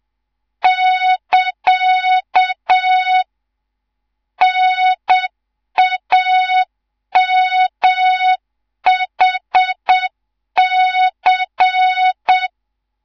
Morse code, heard as a keyed tone, about 7 wpm, 3700 Hz, with noise. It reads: <KA> NAMHC
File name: Morze-HA2MN / hang / 11 GYAKORLAT